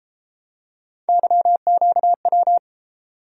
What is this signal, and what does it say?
Morse code, heard as a keyed tone, about 33 wpm, 700 Hz. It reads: YQW